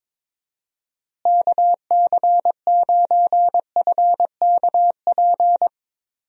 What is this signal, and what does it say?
Morse code, heard as a keyed tone, about 22 wpm, 700 Hz. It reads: KC9FKP
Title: USA Call Signs - 6 Characters - Repeated 2x - 22wpm